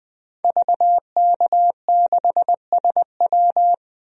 Morse code, heard as a keyed tone, about 20 wpm, 700 Hz. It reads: VK6SW
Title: DXCC-100 Call Signs - 5 Characters - Repeated 2x - 20wpm